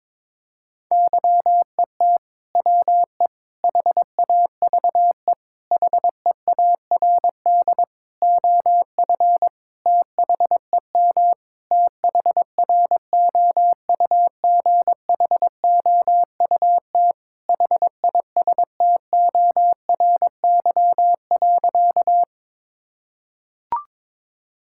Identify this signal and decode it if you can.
Morse code, heard as a keyed tone, about 22 wpm, 700 Hz. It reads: YET WE HAVE HEARD OF THEM THROUGHOUT HISTORY.
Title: Sentences from Top 1000 Words - Review Part 5 - 22wpm